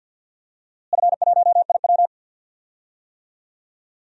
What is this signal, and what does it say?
Morse code, heard as a keyed tone, about 52 wpm, 700 Hz. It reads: F1IW
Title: DXCC-100 Call Signs - 4 Characters - Repeated 3x - 50wpm